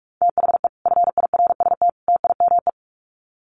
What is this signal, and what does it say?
Morse code, heard as a keyed tone, about 44 wpm, 700 Hz. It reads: THE FIRST TIME
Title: Sets of 3 Words 45wpm